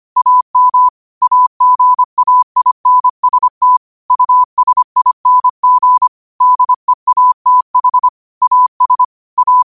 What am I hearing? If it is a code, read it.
Morse code, heard as a keyed tone, about 25 wpm, 1000 Hz. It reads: AM AGAINST USING DEATH AS A
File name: qotdfiles / 25wpm